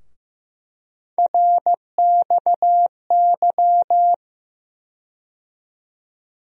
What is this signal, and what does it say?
Morse code, heard as a keyed tone, about 15 wpm, 700 Hz. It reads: RXY
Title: USA Call Sign Postfixes - 3 Letters 15wpm